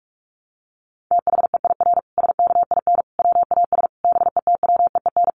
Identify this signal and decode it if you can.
Morse code, heard as a keyed tone, about 44 wpm, 700 Hz. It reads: THEIR SKIN WAS BETWEEN
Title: Sentences from Top 1300 Words 45wpm